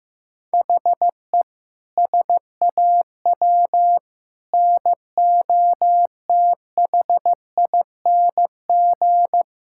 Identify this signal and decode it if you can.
Morse code, heard as a keyed tone, about 16 wpm, 700 Hz. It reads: HE SAW NOTHING
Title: Sentences from Top 1600 Words 15wpm